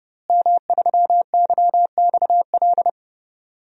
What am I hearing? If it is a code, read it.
Morse code, heard as a keyed tone, about 32 wpm, 700 Hz. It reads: M3YXL
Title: DXCC-100 Call Signs - 5 Characters - Repeated 2x - 30wpm